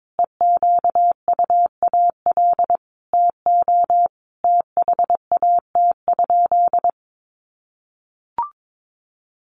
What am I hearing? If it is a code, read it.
Morse code, heard as a keyed tone, about 22 wpm, 700 Hz. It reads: EQUAL TO THAT?
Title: Sentences from Top 1400 Words 22wpm